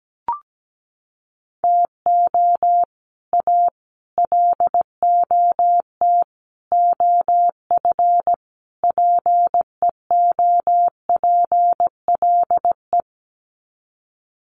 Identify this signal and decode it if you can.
Morse code, heard as a keyed tone, about 17 wpm, 700 Hz. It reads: TO A LOT OF PEOPLE
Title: Sets of 5 Words Encore 17wpm